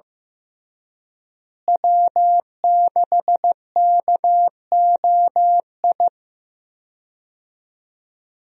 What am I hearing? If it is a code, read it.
Morse code, heard as a keyed tone, about 15 wpm, 700 Hz. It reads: W6KOI
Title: USA Call Signs - 5 Characters - Repeated 3x - 15wpm